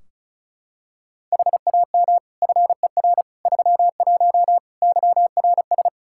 Morse code, heard as a keyed tone, about 34 wpm, 700 Hz. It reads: HAM FER 31 YRS
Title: Practice QSO - Exchange 3 - More Info - 35wpm